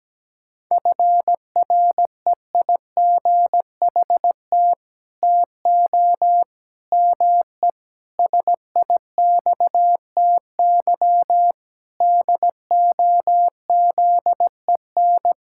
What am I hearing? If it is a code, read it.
Morse code, heard as a keyed tone, about 17 wpm, 700 Hz. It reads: FREIGHT TO ME SIXTY DOZEN
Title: Pangram Challenge v2 - 17wpm